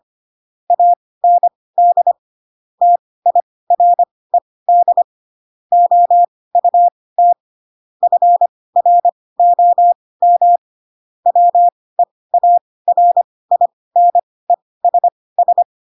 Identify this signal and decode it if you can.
Morse code, heard as a keyed tone, about 25 wpm, spaced as for 18 wpm, 700 Hz. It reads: AND TIRED OUT FROM WEARINESS
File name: Pinocchio / Pinocchio ch6 12 25 ik1hge